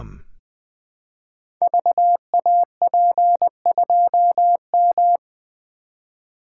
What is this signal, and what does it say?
Morse code, heard as a keyed tone, about 20 wpm, 700 Hz. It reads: VAP2M
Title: Code Groups - 5 Letter:Numbers - 20wpm